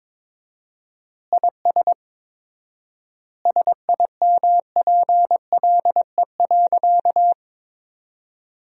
Morse code, heard as a keyed tone, about 22 wpm, 700 Hz. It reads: IS SIMPLE.
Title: 4x Word-Spacing - Sentences from Top 400 Words - 22wpm